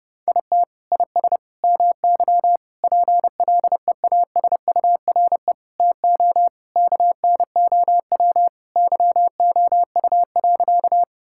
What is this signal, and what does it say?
Morse code, heard as a keyed tone, about 30 wpm, 700 Hz. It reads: IT IS MY PLEASURE TO KNOW YOU.
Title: Sentences from Top 1600 Words 30wpm